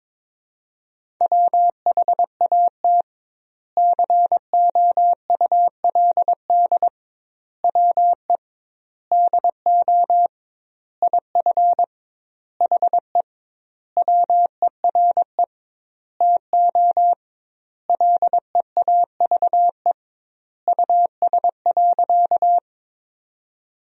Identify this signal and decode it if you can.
Morse code, heard as a keyed tone, about 23 wpm, 700 Hz. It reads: WHAT COULD WE DO IF HE WERE TO LEAVE US.
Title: 2x Word-Spacing - Sentences from Top 400 Words - 22wpm